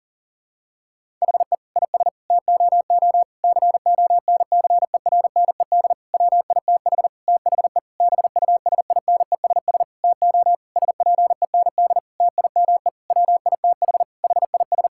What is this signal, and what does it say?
Morse code, heard as a keyed tone, about 40 wpm, 700 Hz. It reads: HE IS TOO CONCERNED WITH THE BUSINESS TO SPEND TIME WITH HIS
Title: Sentences from Top 1000 Words - Review Part 6 - 40wpm